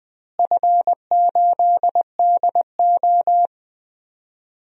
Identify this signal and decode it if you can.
Morse code, heard as a keyed tone, about 20 wpm, 700 Hz. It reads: F8DO
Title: DXCC-100 Call Signs - 4 Characters - Repeated 3x - 20wpm